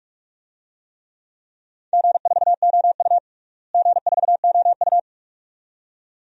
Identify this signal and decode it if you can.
Morse code, heard as a keyed tone, about 46 wpm, 700 Hz. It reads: G4OU G4OU
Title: DXCC-100 Call Signs - 4 Characters - Repeated 2x - 45wpm